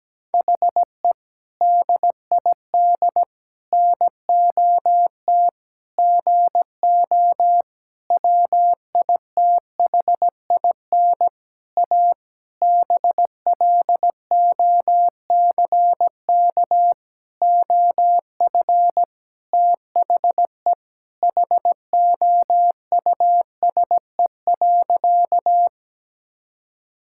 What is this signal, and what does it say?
Morse code, heard as a keyed tone, about 17 wpm, 700 Hz. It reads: HE DID NOT GO WITHIN A BLOCK OF THE HOUSE.
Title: Sentences from Top 1600 Words 17wpm